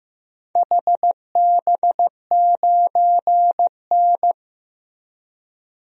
Morse code, heard as a keyed tone, about 15 wpm, 700 Hz. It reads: HB9N
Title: DXCC-100 Call Signs - 4 Characters - Repeated 3x - 15wpm